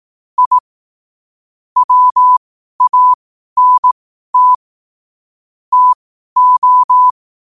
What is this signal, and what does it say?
Morse code, heard as a keyed tone, about 18 wpm, spaced as for 12 wpm, 1000 Hz. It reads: I WANT TO
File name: qotdfiles / 12wpm